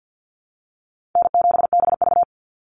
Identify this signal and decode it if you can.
Morse code, heard as a keyed tone, about 50 wpm, 700 Hz. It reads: N7BV
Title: Call Signs - 50wpm